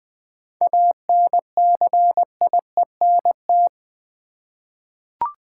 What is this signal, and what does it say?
Morse code, heard as a keyed tone, about 20 wpm, 700 Hz. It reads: ANCIENT
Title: 7 Letter Words - First 3 Letters Given - 20wpm